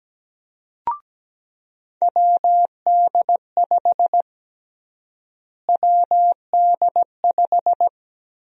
Morse code, heard as a keyed tone, about 17 wpm, 700 Hz. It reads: WD5 WD5
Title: USA Call Sign Prefixes - Repeated 2x - 17wpm